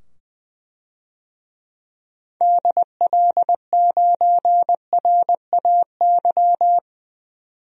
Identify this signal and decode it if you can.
Morse code, heard as a keyed tone, about 20 wpm, 700 Hz. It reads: DL9RAY
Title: DXCC-100 Call Signs - 6 Characters - 20wpm